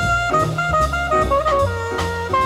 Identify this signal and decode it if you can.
There is no code.